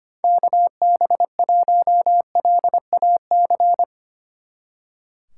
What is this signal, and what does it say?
Morse code, heard as a keyed tone, about 31 wpm, 700 Hz. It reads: KB1LAC
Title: Speed Racing - USA 2x3 Call Signs 25wpm